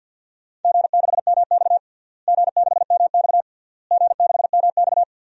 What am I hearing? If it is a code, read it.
Morse code, heard as a keyed tone, about 51 wpm, 700 Hz. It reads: G6MX G6MX G6MX
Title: DXCC-100 Call Signs - 4 Characters - Repeated 3x - 50wpm